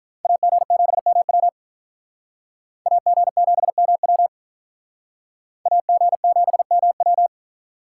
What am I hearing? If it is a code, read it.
Morse code, heard as a keyed tone, about 43 wpm, 700 Hz. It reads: AG7MW AG7MW AG7MW
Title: Speed Racing - USA 2x2 Call Signs 35wpm